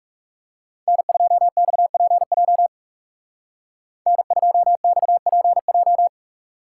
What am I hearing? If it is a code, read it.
Morse code, heard as a keyed tone, about 36 wpm, 700 Hz. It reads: N2XPJ N2XPJ
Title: Speed Racing - USA 1x3 Call Signs 30wpm